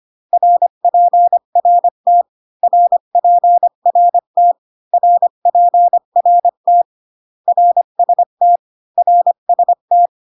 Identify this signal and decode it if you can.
Morse code, heard as a keyed tone, about 25 wpm, 700 Hz. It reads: RPRT RPRT RPRT RST RST